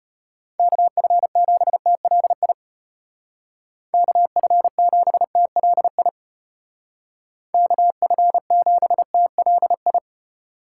Morse code, heard as a keyed tone, about 34 wpm, 700 Hz. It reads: KF7TLI KF7TLI KF7TLI